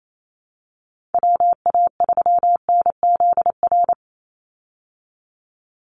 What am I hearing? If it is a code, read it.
Morse code, heard as a keyed tone, about 28 wpm, 690 Hz. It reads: WA3NZR
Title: Call Signs Repeated 3x - 28wpm